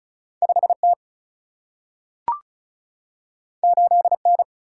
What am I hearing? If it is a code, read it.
Morse code, heard as a keyed tone, about 36 wpm, 700 Hz. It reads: 5 T 8 N
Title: Single-Letter Single-Number 35wpm